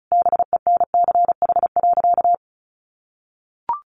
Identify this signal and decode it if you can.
Morse code, heard as a keyed tone, about 34 wpm, 700 Hz. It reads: BENCH.
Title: Sentences from Top 1400 Words 35wpm